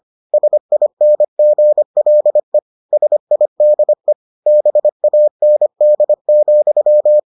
Morse code, heard as a keyed tone, about 25 wpm, 600 Hz. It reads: SINGLE SIDE BAND,